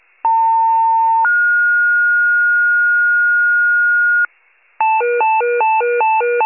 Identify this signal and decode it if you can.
Two-tone page, heard as a one-second tone, then a three-second tone, about 900 then 1430 Hz, with noise. It is heard from 0.2 to 4.3 s.